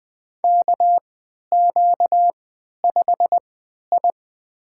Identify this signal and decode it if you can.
Morse code, heard as a keyed tone, about 21 wpm, spaced as for 12 wpm, 700 Hz. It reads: KQ5I
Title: USA Call Signs - 4 Characters - 3x Character Spacing - 20wpm